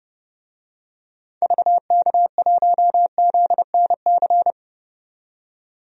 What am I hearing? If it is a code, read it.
Morse code, heard as a keyed tone, about 32 wpm, 700 Hz. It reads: VK1ZNC
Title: DXCC-100 Call Signs - 6 Characters - 30wpm